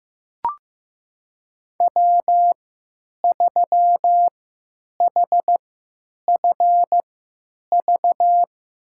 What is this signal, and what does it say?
Morse code, heard as a keyed tone, about 15 wpm, spaced as for 9 wpm, 700 Hz. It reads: W3HFV
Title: USA Call Signs - 5 Characters - 3x Character Spacing - 15wpm